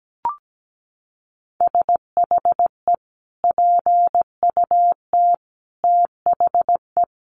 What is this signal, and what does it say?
Morse code, heard as a keyed tone, about 17 wpm, 700 Hz. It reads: SHE PUT THE
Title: Sentences from Top 1400 Words 17wpm